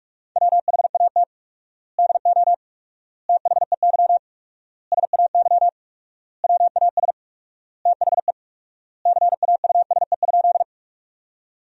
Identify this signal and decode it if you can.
Morse code, heard as a keyed tone, about 46 wpm, 700 Hz. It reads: WHAT DO THEY SAY WAS THE CAUSE?
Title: 4x Word-Spacing - Sentences from Top 200 Words - 45wpm